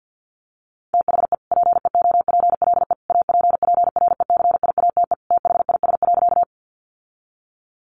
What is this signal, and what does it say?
Morse code, heard as a keyed tone, about 48 wpm, 700 Hz. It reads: THE PEOPLE APPRECIATE THIS.
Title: Sentences from Top 1300 Words 50wpm